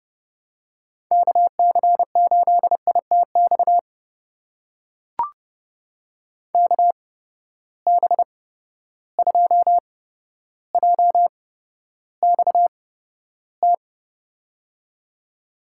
Morse code, heard as a keyed tone, about 29 wpm, spaced as for 13 wpm, 700 Hz. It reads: KC8ITX KB2JXT
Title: USA Call Signs - 6 Characters - 8x Character Spacing - 30wpm